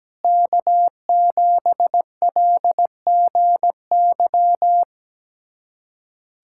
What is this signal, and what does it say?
Morse code, heard as a keyed tone, about 17 wpm, 700 Hz. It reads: K7LGY
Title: USA Callsign DE Callsign - 5 Characters - 17wpm